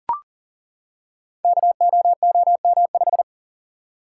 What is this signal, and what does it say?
Morse code, heard as a keyed tone, about 42 wpm, 690 Hz. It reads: KOOM5
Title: Code Groups - 5 Letter:Numbers - 40wpm